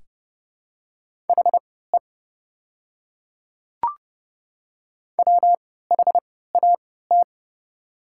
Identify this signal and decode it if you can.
Morse code, heard as a keyed tone, about 29 wpm, spaced as for 17 wpm, 700 Hz. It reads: HE WHAT
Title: 3x Character Spacing - Top 200 Words - 30wpm